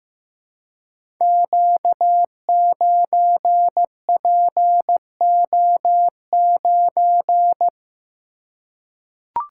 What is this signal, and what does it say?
Morse code, heard as a keyed tone, about 15 wpm, 700 Hz. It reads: Q9PO9
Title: Code Groups - 5 Letter:Numbers - 15wpm